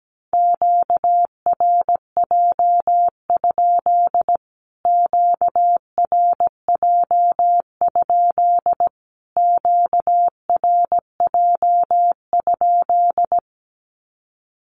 Morse code, heard as a keyed tone, about 17 wpm, 700 Hz. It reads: QRJ? QRJ? QRJ?
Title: Q-Codes Repeated 3x 17wpm